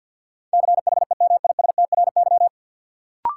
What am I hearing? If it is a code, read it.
Morse code, heard as a keyed tone, about 48 wpm, 700 Hz. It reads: CHEMISTRY
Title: Sentences from Top 1000 Words - Review Part 6 - New Word Drill - 50wpm